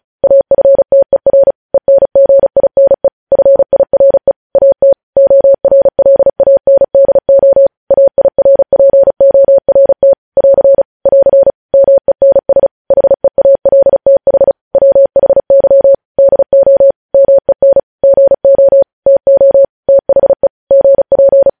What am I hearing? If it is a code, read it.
Morse code, heard as a keyed tone, about 36 wpm, 550 Hz. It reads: AFTER ENGINE FIRE AT ORLANDO AIRPORT <AR> <AR> MENS HEALTH WHY DO MEN GO TO THE GP
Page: news headlines in morse code at 35 wpm